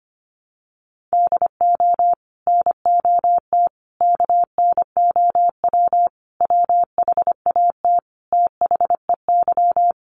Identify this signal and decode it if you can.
Morse code, heard as a keyed tone, about 25 wpm, 700 Hz. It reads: DO NOT KNOW WHAT THEY
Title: Sets of 5 Words 25wpm